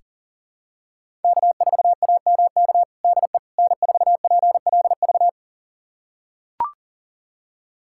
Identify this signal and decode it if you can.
Morse code, heard as a keyed tone, about 40 wpm, 700 Hz. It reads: K4AMK DE N4PLV
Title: USA Callsign DE Callsign - 5 Characters - 40wpm